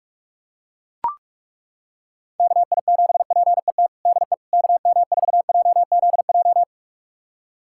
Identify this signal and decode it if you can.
Morse code, heard as a keyed tone, about 43 wpm, 700 Hz. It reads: KI7PET DE KM4JZJ
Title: USA Callsign DE Callsign - 6 Characters - 45wpm